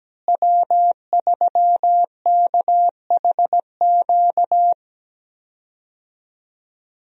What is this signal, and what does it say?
Morse code, heard as a keyed tone, about 17 wpm, 700 Hz. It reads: W3KHQ
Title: USA Call Signs - 5 Characters - Repeated 2x - 17wpm